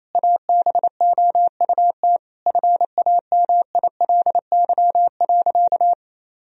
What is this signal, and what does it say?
Morse code, heard as a keyed tone, about 28 wpm, 700 Hz. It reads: ABOUT FAMILY.
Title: Sentences from Top 1000 Words - Review Part 3 - 28wpm